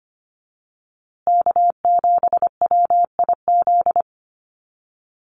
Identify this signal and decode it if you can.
Morse code, heard as a keyed tone, about 25 wpm, 700 Hz. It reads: K7WIZ
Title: USA Call Signs Repeated 2x 25wpm